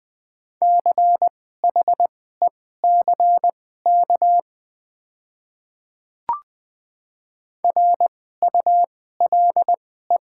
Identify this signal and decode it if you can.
Morse code, heard as a keyed tone, about 20 wpm, spaced as for 14 wpm, 700 Hz. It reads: CHECK RULE
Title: 2x Character Spacing - Top 500 Words - 20wpm